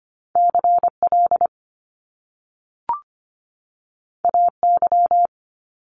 Morse code, heard as a keyed tone, about 25 wpm, 700 Hz. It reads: CL AY
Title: Top Two Letter Combinations in Words 25wpm